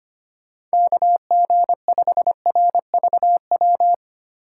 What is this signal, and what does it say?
Morse code, heard as a keyed tone, about 25 wpm, 700 Hz. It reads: KG5RVW
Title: USA Call Signs - 6 Characters - 3x Character Spacing - 25wpm